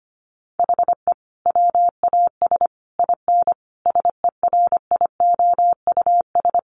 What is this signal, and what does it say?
Morse code, heard as a keyed tone, about 25 wpm, 700 Hz. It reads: HE WAS IN SERIOUS
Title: Sentences from Top 1300 Words 25wpm